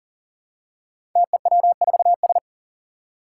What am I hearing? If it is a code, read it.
Morse code, heard as a keyed tone, about 40 wpm, 700 Hz. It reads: TEW4S
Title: Five Challenging Letter-Number Combinations 40wpm